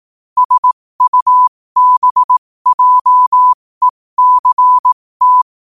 Morse code, heard as a keyed tone, about 18 wpm, spaced as for 15 wpm, 1000 Hz. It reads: SUBJECT
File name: qotdfiles / 15wpm